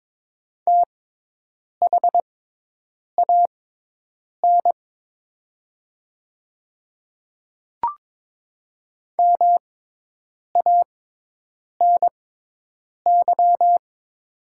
Morse code, heard as a keyed tone, about 22 wpm, spaced as for 8 wpm, 700 Hz. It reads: THAN MANY
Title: 6x Character Spacing - Top 200 Words - 22wpm